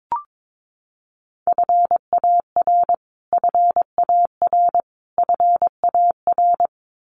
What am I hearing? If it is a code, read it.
Morse code, heard as a keyed tone, about 22 wpm, 700 Hz. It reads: FAR FAR FAR
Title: Top 500 Words Repeated 22wpm